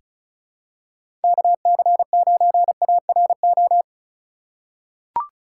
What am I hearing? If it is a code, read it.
Morse code, heard as a keyed tone, about 36 wpm, 700 Hz. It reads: KC9ARO
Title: USA Call Signs - 4x Character Spacing - 35wpm